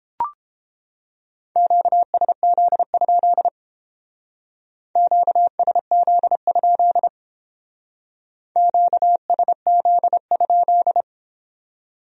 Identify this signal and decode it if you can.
Morse code, heard as a keyed tone, about 29 wpm, 700 Hz. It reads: QSZ? QSZ? QSZ?